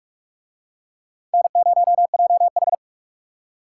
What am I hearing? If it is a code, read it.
Morse code, heard as a keyed tone, about 49 wpm, 700 Hz. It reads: N0JH